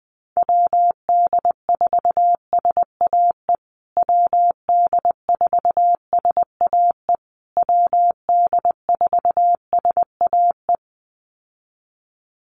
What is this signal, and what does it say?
Morse code, heard as a keyed tone, about 20 wpm, 700 Hz. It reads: WD4SAE WD4SAE WD4SAE
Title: USA Call Signs Repeated 3x 20wpm